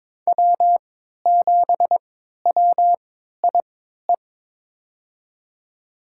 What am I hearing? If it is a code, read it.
Morse code, heard as a keyed tone, about 21 wpm, spaced as for 12 wpm, 700 Hz. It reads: W7WIE
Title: USA Call Signs - 5 Characters - 3x Character Spacing - 22wpm